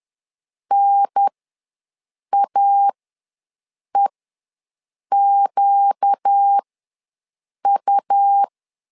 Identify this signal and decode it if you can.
Morse code, heard as a keyed tone, about 10 wpm, spaced as for 6 wpm, 800 Hz. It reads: NAEQU
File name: LEZIONI / 4 Q1 ws 10wpm 2538